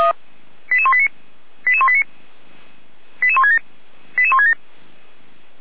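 Five-tone selective call (ZVEI), heard as quick runs of five tones, tones of about 70 ms. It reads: 88189 88189 88167 88167
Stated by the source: ZVEI five-tone call received by SDR.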